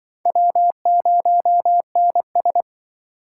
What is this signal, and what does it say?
Morse code, heard as a keyed tone, about 25 wpm, 700 Hz. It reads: W0NS